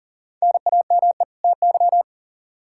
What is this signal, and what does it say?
Morse code, heard as a keyed tone, about 40 wpm, 700 Hz. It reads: NAME TY
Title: Practice QSO - Exchange 1 - Beginning - 40wpm